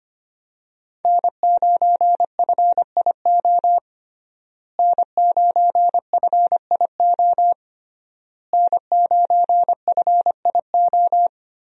Morse code, heard as a keyed tone, about 25 wpm, 700 Hz. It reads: N9FIO N9FIO N9FIO
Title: USA Call Signs - 5 Characters - Repeated 3x - 25wpm